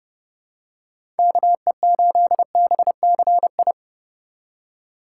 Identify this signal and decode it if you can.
Morse code, heard as a keyed tone, about 30 wpm, 700 Hz. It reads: KE8BCI
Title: USA Call Signs - 6 Characters - 4x Character Spacing - 30wpm